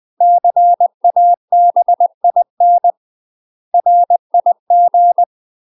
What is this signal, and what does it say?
Morse code, heard as a keyed tone, about 20 wpm, 700 Hz. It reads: CABIN RIG
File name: UK MASTER QSOs TRAINING / 20WPM / UK MASTER QSOs 0037